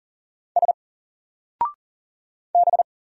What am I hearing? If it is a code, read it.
Morse code, heard as a keyed tone, about 39 wpm, 700 Hz. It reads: S B
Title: Single Letter-Number 40wpm